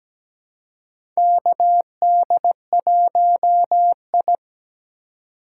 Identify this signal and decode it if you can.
Morse code, heard as a keyed tone, about 17 wpm, 700 Hz. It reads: KD1I
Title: USA Call Signs - 4x Character Spacing - 17wpm